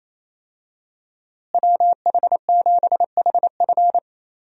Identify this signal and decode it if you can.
Morse code, heard as a keyed tone, about 28 wpm, 700 Hz. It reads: WH7HF
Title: USA Call Signs - 5 Characters - Repeated 3x - 28wpm